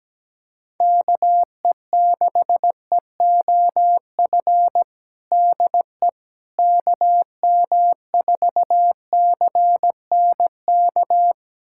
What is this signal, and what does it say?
Morse code, heard as a keyed tone, about 17 wpm, 700 Hz. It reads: KE6EOF DE KM4CNK
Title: USA Callsign DE Callsign - 6 Characters - 17wpm